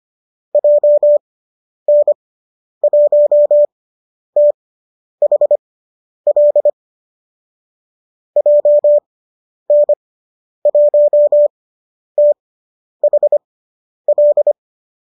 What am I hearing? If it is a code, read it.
Morse code, heard as a keyed tone, about 25 wpm, spaced as for 10 wpm, 590 Hz. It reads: JN1THL JN1THL